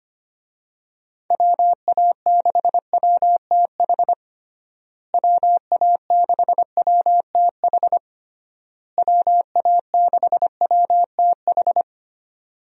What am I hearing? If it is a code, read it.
Morse code, heard as a keyed tone, about 25 wpm, 700 Hz. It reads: WA6WTH WA6WTH WA6WTH
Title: USA Call Signs - 6 Characters - Repeated 3x - 25wpm